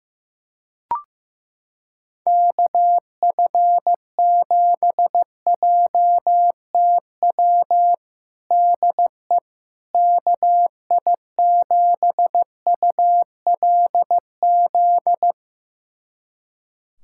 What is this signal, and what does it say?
Morse code, heard as a keyed tone, about 15 wpm, 700 Hz. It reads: KF7JTW DE KI7ULZ
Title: USA Callsign DE Callsign 15wpm